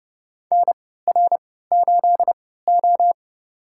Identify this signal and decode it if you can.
Morse code, heard as a keyed tone, about 30 wpm, spaced as for 17 wpm, 700 Hz. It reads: NR8O
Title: USA Call Signs - 4 Characters - 3x Character Spacing - 30wpm